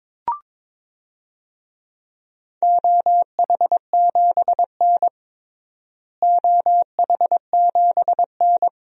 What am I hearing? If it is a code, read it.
Morse code, heard as a keyed tone, about 22 wpm, 700 Hz. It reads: OH7N OH7N
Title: DXCC-100 Call Signs - 4 Characters - Repeated 3x - 22wpm